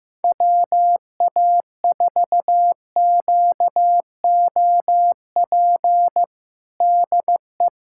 Morse code, heard as a keyed tone, about 15 wpm, 700 Hz. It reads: WA4QOP DE
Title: USA Callsign DE Callsign 15wpm